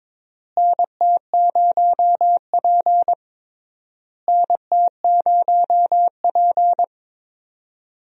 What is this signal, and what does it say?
Morse code, heard as a keyed tone, about 22 wpm, 700 Hz. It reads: NT0P NT0P
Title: USA Call Signs - 4 Characters - Repeated 3x - 22wpm